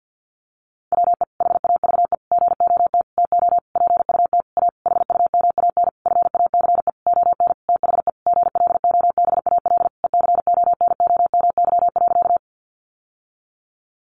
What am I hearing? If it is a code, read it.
Morse code, heard as a keyed tone, about 50 wpm, 700 Hz. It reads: WE HAVE GOT TO PUT A HUMAN FACE ON THE GLOBAL ECONOMY.
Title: Sentences from Top 1200 Words 50wpm